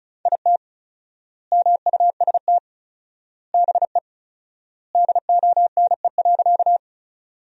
Morse code, heard as a keyed tone, about 36 wpm, 700 Hz. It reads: IT MUST BE DONE.
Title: 4x Word-Spacing - Sentences from Top 500 Words - 35wpm